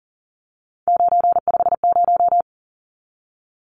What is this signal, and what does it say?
Morse code, heard as a keyed tone, about 40 wpm, 700 Hz. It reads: 950
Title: Numbers - 3-Digit 40wpm